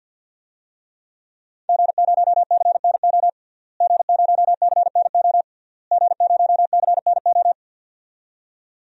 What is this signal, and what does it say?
Morse code, heard as a keyed tone, about 50 wpm, 700 Hz. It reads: G0CNO G0CNO G0CNO
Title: DXCC-100 Call Signs - 5 Characters - Repeated 3x - 50wpm